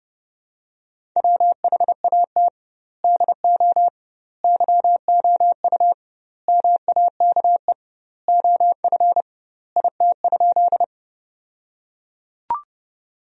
Morse code, heard as a keyed tone, about 31 wpm, 700 Hz. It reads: WHAT DO YOU MAKE OF IT?
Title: 2x Word-Spacing - Sentences from Top 100 Words - 30wpm